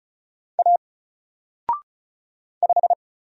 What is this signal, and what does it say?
Morse code, heard as a keyed tone, about 34 wpm, 700 Hz. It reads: A 5
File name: Single-Letter-Number-35wpm